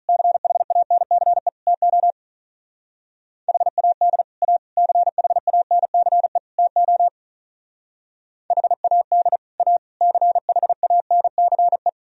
Morse code, heard as a keyed tone, about 43 wpm, 700 Hz. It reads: CHANCE TO HAD A CHANCE TO HAD A CHANCE
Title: Speed-Racing - Sets of 4 Words 35wpm